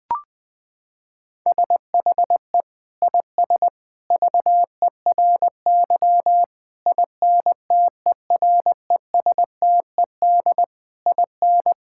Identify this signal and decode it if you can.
Morse code, heard as a keyed tone, about 21 wpm, 700 Hz. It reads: SHE IS VERY INTERESTED IN
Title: Sentences from Top 1000 Words - Review Part 6 - 20wpm